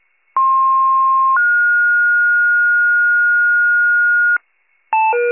Two-tone page, heard as a one-second tone, then a three-second tone, about 1060 then 1430 Hz, with noise. It is heard from 0.4 to 4.4 s.
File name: wcalert